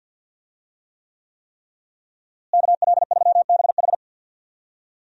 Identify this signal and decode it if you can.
Morse code, heard as a keyed tone, about 46 wpm, 700 Hz. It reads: KL3BH
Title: USA Call Signs - 5 Characters - Repeated 3x - 50wpm